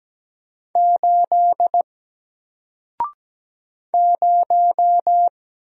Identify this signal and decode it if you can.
Morse code, heard as a keyed tone, about 17 wpm, 700 Hz. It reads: 8 0
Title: Single Character (Letter:Number:Punctuation:Prosign) 17wpm